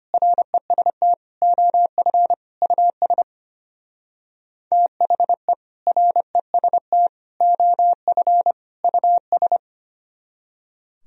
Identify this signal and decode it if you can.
Morse code, heard as a keyed tone, about 33 wpm, 700 Hz. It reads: REST OF US THE REST OF US